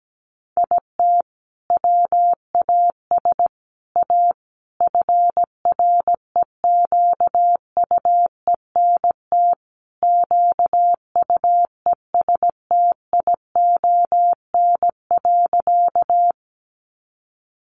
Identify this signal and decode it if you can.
Morse code, heard as a keyed tone, about 17 wpm, 700 Hz. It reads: IT WAS A FREQUENT QUESTION.
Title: Sentences from Top 1400 Words 17wpm